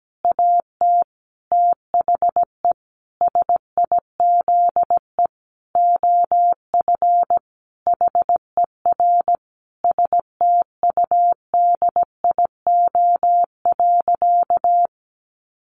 Morse code, heard as a keyed tone, about 17 wpm, 700 Hz. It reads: AT THE SIZE OF HER STUDIO.
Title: Sentences from Top 1200 Words 17wpm